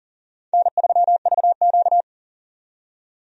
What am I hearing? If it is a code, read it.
Morse code, heard as a keyed tone, about 40 wpm, 700 Hz. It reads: N3VQ
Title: USA Call Signs - 4 Characters - 40wpm